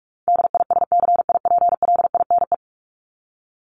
Morse code, heard as a keyed tone, about 44 wpm, 700 Hz. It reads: DISCIPLINE
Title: Top 751-1000 Words 45wpm